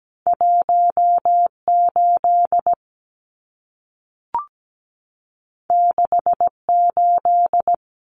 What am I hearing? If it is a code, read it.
Morse code, heard as a keyed tone, about 17 wpm, 700 Hz. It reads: 18 68
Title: Numbers - 2-Digit 17wpm